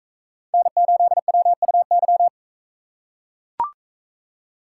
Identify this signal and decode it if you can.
Morse code, heard as a keyed tone, about 43 wpm, 700 Hz. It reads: N8WUY